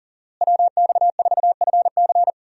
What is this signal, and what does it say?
Morse code, heard as a keyed tone, about 40 wpm, 700 Hz. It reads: WX4FC
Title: USA Call Signs - 5 Characters - Repeated 3x - 40wpm